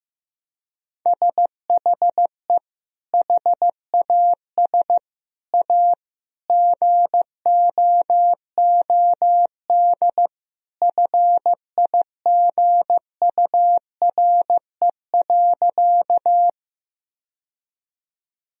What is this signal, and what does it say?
Morse code, heard as a keyed tone, about 15 wpm, 700 Hz. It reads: SHE HAS A GOOD FIGURE.
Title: Sentences from Top 1000 Words - Review Part 2 - 15wpm